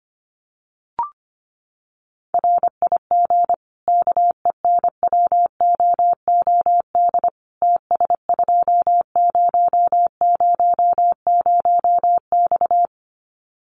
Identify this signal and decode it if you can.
Morse code, heard as a keyed tone, about 25 wpm, 700 Hz. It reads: RIG KENWOOD TS2000X
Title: Common QSO Phrases 25wpm